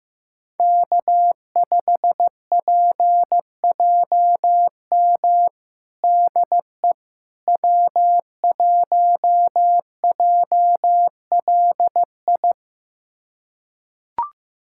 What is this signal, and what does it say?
Morse code, heard as a keyed tone, about 15 wpm, 700 Hz. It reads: K5PJM DE W1JLI
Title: USA Callsign DE Callsign - 5 Characters - 15wpm